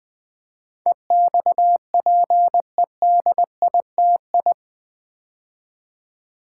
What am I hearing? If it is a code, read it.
Morse code, heard as a keyed tone, about 20 wpm, 700 Hz. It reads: EXPEDITI
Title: Build A Word - 4 Letters At A Time - 10-Letter Words - 20wpm